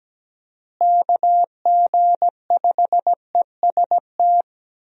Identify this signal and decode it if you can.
Morse code, heard as a keyed tone, about 17 wpm, 700 Hz. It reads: KG5EST
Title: USA Call Signs - 6 Characters - Repeated 3x - 17wpm